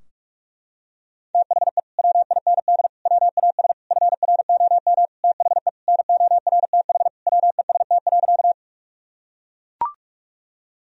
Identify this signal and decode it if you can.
Morse code, heard as a keyed tone, about 42 wpm, 700 Hz. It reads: THE WIND WAS FROM THE NORTH WEST.
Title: Sentences from Top 1000 Words - Review Part 2 - 45wpm